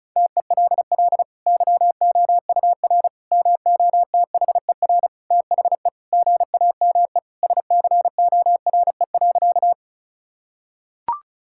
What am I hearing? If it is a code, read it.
Morse code, heard as a keyed tone, about 36 wpm, 700 Hz. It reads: TELL YOUR MOTHER THE GAME SCORE.
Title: Sentences from Top 1600 Words 35wpm